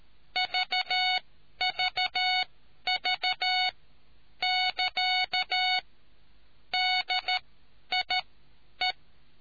Morse code, heard as a keyed tone, about 13 wpm, spaced as for 10 wpm, 2200 Hz. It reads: VVV <KA> DIE